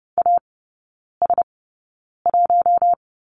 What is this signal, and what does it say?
Morse code, heard as a keyed tone, about 30 wpm, spaced as for 9 wpm, 700 Hz. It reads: AS1
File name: Lesson 19e - Letter w - Three Characters 30wpm